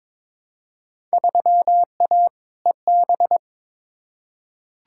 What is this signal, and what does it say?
Morse code, heard as a keyed tone, about 22 wpm, 700 Hz. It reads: 3A EB
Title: Field Day Exchanges Encore - 22wpm